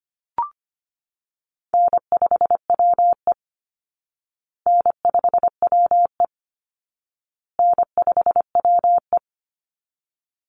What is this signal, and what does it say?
Morse code, heard as a keyed tone, about 25 wpm, 700 Hz. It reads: N5WE N5WE N5WE
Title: Call Signs Repeated 2x - 25wpm